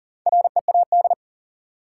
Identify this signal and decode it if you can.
Morse code, heard as a keyed tone, about 40 wpm, 700 Hz. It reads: READ
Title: Sentences from Top 200 Words - New Word Drill - 40wpm